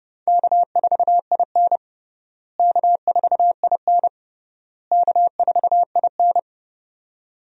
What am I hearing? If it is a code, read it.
Morse code, heard as a keyed tone, about 30 wpm, 700 Hz. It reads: K4IN K4IN K4IN
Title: USA Call Signs - 4 Characters - Repeated 3x - 30wpm